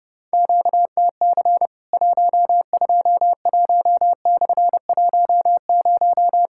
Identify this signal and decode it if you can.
Morse code, heard as a keyed tone, about 30 wpm, 700 Hz. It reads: QTC 121/10
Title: Worked All Europe DX Contest - QTC Exchange - 30wpm